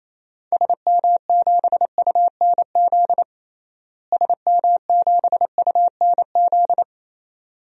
Morse code, heard as a keyed tone, about 28 wpm, 700 Hz. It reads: SM7UNZ SM7UNZ
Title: DXCC-100 Call Signs - 6 Characters - Repeated 2x - 28wpm